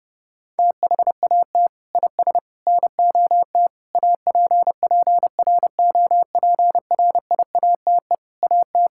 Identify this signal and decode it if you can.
Morse code, heard as a keyed tone, about 30 wpm, 700 Hz. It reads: THAT IS NOT APPROPRIATE AT
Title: Sentences from Top 1000 Words - Review Part 5 - 30wpm